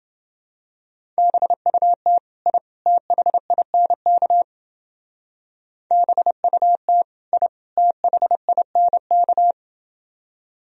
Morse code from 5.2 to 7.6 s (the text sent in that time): BUT I